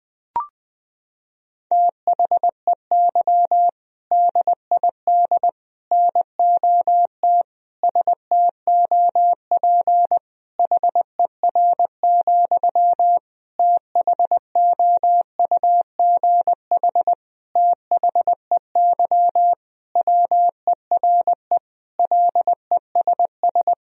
Morse code, heard as a keyed tone, about 20 wpm, 700 Hz. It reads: THEY DID NOT STOP HER, THOUGH THEY WERE LESS
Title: Sentences from Top 1000 Words - Review Part 2 - 20wpm